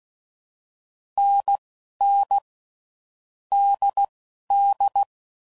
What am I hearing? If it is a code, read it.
Morse code, heard as a keyed tone, about 16 wpm, spaced as for 11 wpm, 800 Hz. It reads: NN DD
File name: Mixed / Short Runs / NDB NDB6